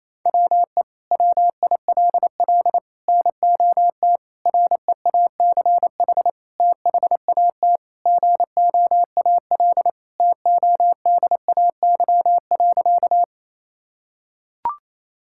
Morse code, heard as a keyed tone, about 28 wpm, 700 Hz. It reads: WE WILL NOT REACH THAT GOAL TODAY.
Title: Sentences from Top 1000 Words - Review Part 5 - 28wpm